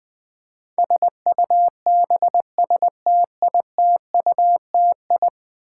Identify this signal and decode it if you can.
Morse code, heard as a keyed tone, about 20 wpm, 700 Hz. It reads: SUBSTITUTI
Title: Build A Word - 5 Letters At A Time - 12-Letter Words - 20wpm